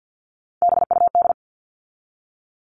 Morse code, heard as a keyed tone, about 50 wpm, 700 Hz. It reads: BUD
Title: All Cut Numbers - 3-Digit 50wpm